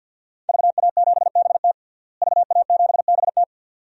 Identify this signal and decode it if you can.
Morse code, heard as a keyed tone, about 46 wpm, 700 Hz. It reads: VA7BT VA7BT
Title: DXCC-100 Call Signs - 5 Characters - Repeated 2x - 50wpm